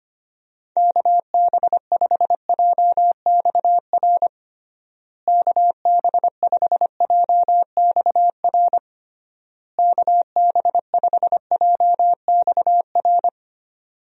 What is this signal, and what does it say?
Morse code, heard as a keyed tone, about 25 wpm, 700 Hz. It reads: KB5JXR KB5JXR KB5JXR
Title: USA Call Signs - 6 Characters - Repeated 3x - 25wpm